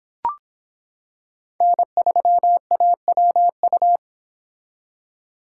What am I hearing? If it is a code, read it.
Morse code, heard as a keyed tone, about 25 wpm, 700 Hz. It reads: N3AWU